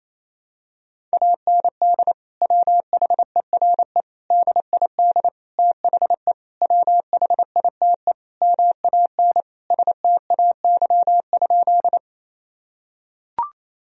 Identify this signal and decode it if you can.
Morse code, heard as a keyed tone, about 28 wpm, 700 Hz. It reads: AND WHERE DID THE WHITE MAN STAY?
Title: Sentences from Top 1000 Words - Review Part 1 - 28wpm